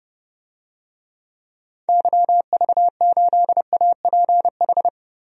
Morse code, heard as a keyed tone, about 30 wpm, 700 Hz. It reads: YV8APH
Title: DXCC-100 Call Signs - 6 Characters - Repeated 3x - 30wpm